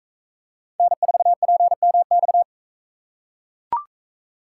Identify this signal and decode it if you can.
Morse code, heard as a keyed tone, about 40 wpm, 700 Hz. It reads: N4PMX